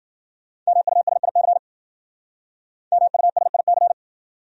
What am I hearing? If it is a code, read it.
Morse code, heard as a keyed tone, about 50 wpm, 700 Hz. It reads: MUSIC MUSIC
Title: Speed-Racing Top 500 Words 40wpm